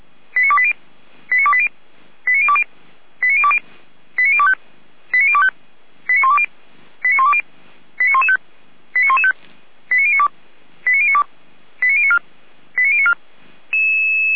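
Five-tone selective call (ZVEI), heard as quick runs of five tones, tones of about 70 ms. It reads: 89280 89280 89020 89020 89025 89025 89120 89120 89115 89115 89092 89092 89094 89094
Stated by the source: ZVEI five-tone selective call received on an SDR by Feuerwehr Orlamünde.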